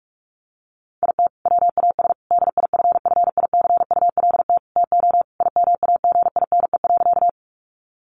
Morse code, heard as a keyed tone, about 44 wpm, 700 Hz. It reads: IT WAS DIFFICULT TO IMAGINE.
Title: Sentences from Top 1400 Words 45wpm